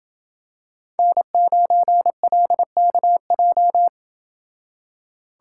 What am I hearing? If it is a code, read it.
Morse code, heard as a keyed tone, about 27 wpm, 700 Hz. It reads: N9LKJ